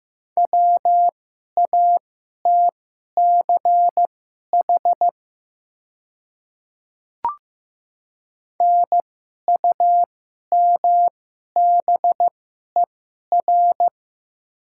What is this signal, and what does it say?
Morse code, heard as a keyed tone, about 15 wpm, spaced as for 11 wpm, 700 Hz. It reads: WATCH NUMBER
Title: 2x Character Spacing - Top 500 Words - 15wpm